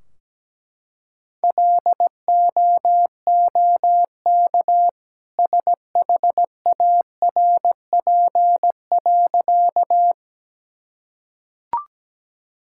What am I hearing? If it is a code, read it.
Morse code, heard as a keyed tone, about 17 wpm, 700 Hz. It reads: LOOK SHARP.
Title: Sentences from Top 1600 Words 17wpm